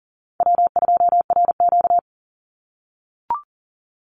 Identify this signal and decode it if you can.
Morse code, heard as a keyed tone, about 40 wpm, 700 Hz. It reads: W2RQ
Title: Easy Call Signs - 40wpm